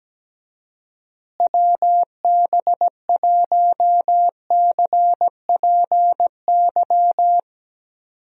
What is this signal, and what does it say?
Morse code, heard as a keyed tone, about 17 wpm, 700 Hz. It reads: WB1CPY
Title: USA Call Signs - 6 Characters - Repeated 2x - 17wpm